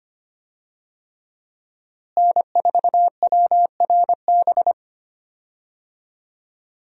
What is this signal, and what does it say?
Morse code, heard as a keyed tone, about 25 wpm, 700 Hz. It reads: N4WRB
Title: USA Call Signs - 5 Characters - Repeated 3x - 25wpm